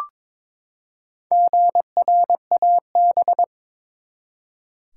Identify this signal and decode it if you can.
Morse code, heard as a keyed tone, about 22 wpm, 700 Hz. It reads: GRAB
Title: Four Letter Words - 22wpm